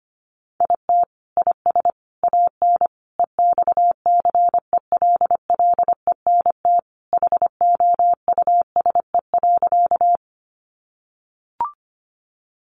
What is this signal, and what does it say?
Morse code, heard as a keyed tone, about 25 wpm, 700 Hz. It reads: IT IS AN EXCELLENT HOUSE.
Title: Sentences from Top 1000 Words 25wpm